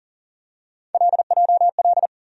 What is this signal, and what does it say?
Morse code, heard as a keyed tone, about 40 wpm, 700 Hz. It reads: LJL
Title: USA Call Sign Postfixes - 3 Letters 40wpm